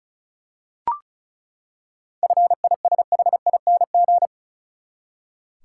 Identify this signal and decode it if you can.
Morse code, heard as a keyed tone, about 36 wpm, 700 Hz. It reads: FISHING
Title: Seven Letter Words - 35wpm